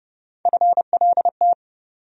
Morse code, heard as a keyed tone, about 29 wpm, 700 Hz. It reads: FLT
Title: USA Call Sign Postfixes - 3 Letters 30wpm